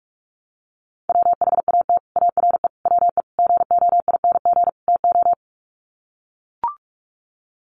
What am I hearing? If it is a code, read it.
Morse code, heard as a keyed tone, about 44 wpm, 700 Hz. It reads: WHAT ARE WE GOING TO
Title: Sets of 5 Words 45wpm